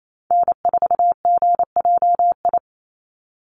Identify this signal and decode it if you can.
Morse code, heard as a keyed tone, about 28 wpm, 700 Hz. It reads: N4GJI
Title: USA Call Signs 28wpm